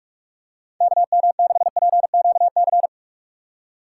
Morse code, heard as a keyed tone, about 47 wpm, 700 Hz. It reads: KM6PQC
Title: USA Call Signs - 6 Characters - 3x Character Spacing - 45wpm